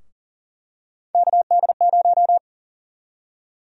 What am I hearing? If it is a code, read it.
Morse code, heard as a keyed tone, about 38 wpm, 700 Hz. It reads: KD0
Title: USA Call Sign Prefixes - Repeated 2x - 40wpm